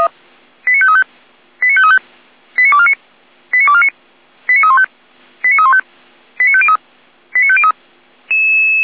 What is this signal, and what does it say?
Five-tone selective call (ZVEI), heard as quick runs of five tones, tones of about 70 ms. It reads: 89535 89535 89249 89249 89315 89315 89503 89503